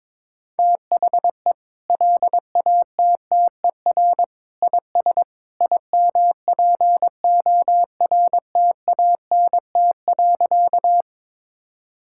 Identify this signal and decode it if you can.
Morse code, heard as a keyed tone, about 22 wpm, 700 Hz. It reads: THE LATTER IS IMPORTANT.
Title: Sentences from Top 1600 Words 22wpm